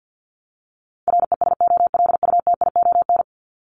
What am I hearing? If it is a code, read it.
Morse code, heard as a keyed tone, about 48 wpm, 700 Hz. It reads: RESOLUTION
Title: Top 1501-1750 Words 50wpm